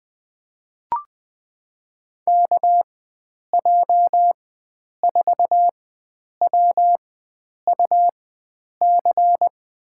Morse code, heard as a keyed tone, about 20 wpm, spaced as for 9 wpm, 700 Hz. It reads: KJ4WUC
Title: USA Call Signs - 4x Character Spacing - 20wpm